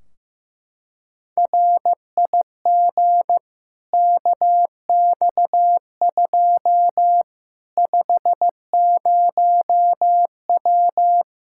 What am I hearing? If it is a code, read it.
Morse code, heard as a keyed tone, about 15 wpm, 700 Hz. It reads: RIG KX2 50W
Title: K1USN SST QSO Practice 15wpm